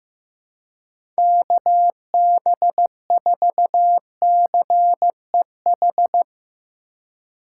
Morse code, heard as a keyed tone, about 15 wpm, 700 Hz. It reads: KB4CEH
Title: USA Call Signs - 6 Characters - 4x Character Spacing - 15wpm